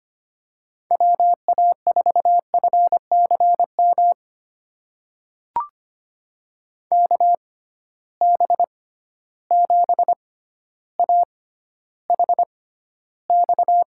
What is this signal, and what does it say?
Morse code, heard as a keyed tone, about 25 wpm, spaced as for 13 wpm, 700 Hz. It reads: WA4FCM KB7AHX
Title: USA Call Signs - 6x Character Spacing - 25wpm